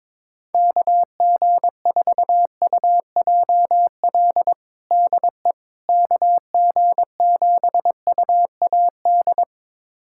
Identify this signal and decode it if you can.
Morse code, heard as a keyed tone, about 22 wpm, 700 Hz. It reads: KG4UJL DE KG7UAD
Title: USA Callsign DE Callsign 22wpm